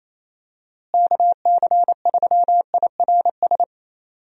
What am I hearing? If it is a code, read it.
Morse code, heard as a keyed tone, about 27 wpm, 700 Hz. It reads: KC3IRS
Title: USA Call Signs - 6 Characters - Repeated 3x - 28wpm